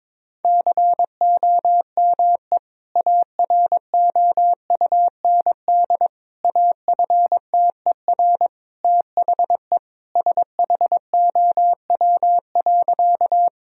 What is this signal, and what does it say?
Morse code, heard as a keyed tone, about 22 wpm, 700 Hz. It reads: COME AROUND AFTER THE SHOW.
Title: Sentences from Top 1000 Words - Review Part 3 - 22wpm